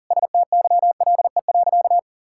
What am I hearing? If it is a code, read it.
Morse code, heard as a keyed tone, about 40 wpm, 690 Hz. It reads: STYLE.
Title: Sentences from Top 1000 Words - Review Part 5 - 40wpm